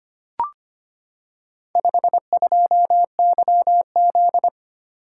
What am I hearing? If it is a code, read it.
Morse code, heard as a keyed tone, about 25 wpm, 700 Hz. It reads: 52YZ
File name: practice / 25wpm / Four Letter-Number Combinations - 25wpm